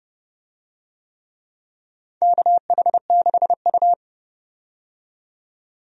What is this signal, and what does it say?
Morse code, heard as a keyed tone, about 29 wpm, 700 Hz. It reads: KH6U
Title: DXCC-100 Call Signs - 4 Characters - 30wpm